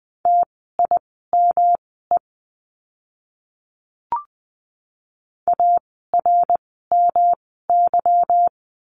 Morse code, heard as a keyed tone, about 20 wpm, spaced as for 15 wpm, 700 Hz. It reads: TIME ARMY
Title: 146 Lesson 24f - Letter y - Words 20wpm